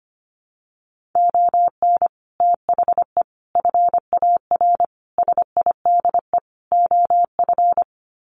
Morse code, heard as a keyed tone, about 25 wpm, 700 Hz. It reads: ON THE FAR SIDE OF
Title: Sets of 5 Words 25wpm